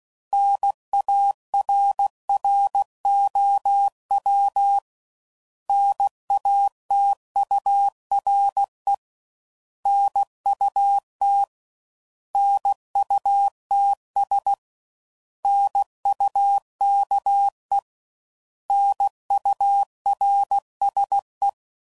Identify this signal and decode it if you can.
Morse code, heard as a keyed tone, about 16 wpm, 800 Hz. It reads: NARROW NATURE NUT NUTS NUKE NURSE
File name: Prefixes / By First Letter / Ns